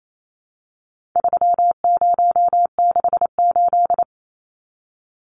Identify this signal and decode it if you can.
Morse code, heard as a keyed tone, about 28 wpm, 700 Hz. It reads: 3068
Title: Numbers - 4-Digit 28wpm